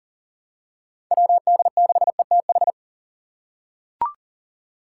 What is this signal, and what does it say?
Morse code, heard as a keyed tone, about 40 wpm, 700 Hz. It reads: WD6ETH
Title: USA Call Signs - 6 Characters - 4x Character Spacing - 40wpm